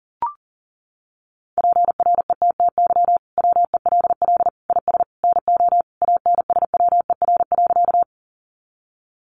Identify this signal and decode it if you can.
Morse code, heard as a keyed tone, about 40 wpm, 700 Hz. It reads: PRETTY WELL IS NO ANSWER.
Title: Sentences from Top 1000 Words 40wpm